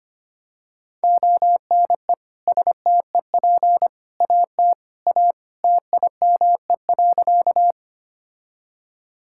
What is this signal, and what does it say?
Morse code, heard as a keyed tone, about 25 wpm, 700 Hz. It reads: ONE STEP AT A TIME.
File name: practice / 25wpm / Sentences with 3-Word Phrases - 25wpm